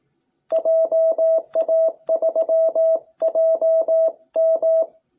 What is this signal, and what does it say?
Morse code, heard as a keyed tone, about 18 wpm, 660 Hz. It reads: JA3JM